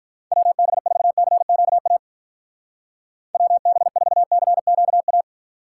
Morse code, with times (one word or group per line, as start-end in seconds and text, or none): 0.3-2.0 WB4CQA
3.3-5.2 WB4CQA